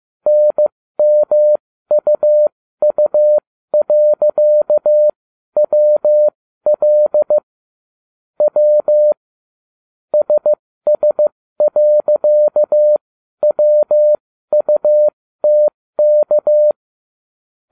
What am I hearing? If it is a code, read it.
Morse code, heard as a keyed tone, about 15 wpm, spaced as for 12 wpm, 600 Hz. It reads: NMUU.WL W SS.WUTK